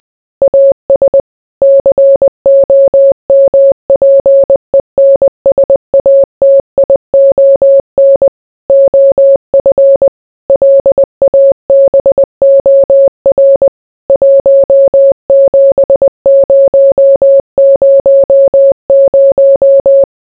Morse code, heard as a keyed tone, about 20 wpm, 550 Hz. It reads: AS COMPENSATION OF LABOR 17000